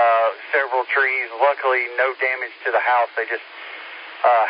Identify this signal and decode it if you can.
Emergency Alert System (SAME) tones, heard as harsh buzzing bursts, with speech. It is heard from 0.9 to 4.0 s.